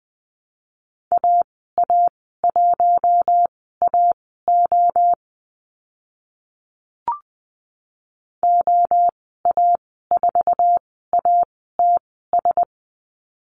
Morse code, heard as a keyed tone, about 20 wpm, spaced as for 14 wpm, 700 Hz. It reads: AA1AO OA4ATS
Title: 069 Lesson 12g - Letter d - Call Signs 20wpm